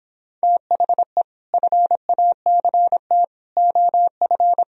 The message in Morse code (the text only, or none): THE FACT OF